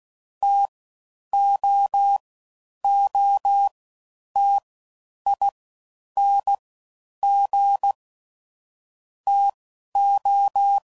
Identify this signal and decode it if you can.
Morse code, heard as a keyed tone, about 16 wpm, 800 Hz. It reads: T O O T I N G T O